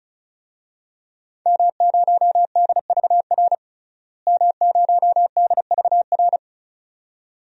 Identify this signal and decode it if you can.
Morse code, heard as a keyed tone, about 34 wpm, 700 Hz. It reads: M0DVR M0DVR
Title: DXCC-100 Call Signs - 5 Characters - Repeated 2x - 35wpm